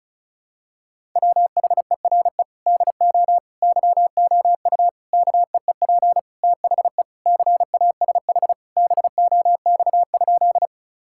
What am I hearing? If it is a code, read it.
Morse code, heard as a keyed tone, about 34 wpm, 700 Hz. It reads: WHERE DO YOU KEEP THE CASH BOX?
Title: Sentences from Top 1000 Words - Review Part 5 - 35wpm